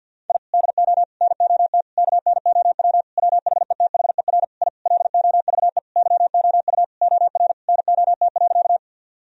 Morse code, heard as a keyed tone, about 49 wpm, 700 Hz. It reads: I DO NOT KNOW WHETHER I LOVE YOU OR NOT.
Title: Sentences from Top 1000 Words - Review Part 3 - 50wpm